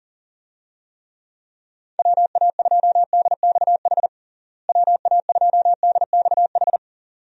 Morse code, heard as a keyed tone, about 40 wpm, 700 Hz. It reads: WA2DXH WA2DXH
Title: USA Call Signs - 6 Characters - Repeated 2x - 40wpm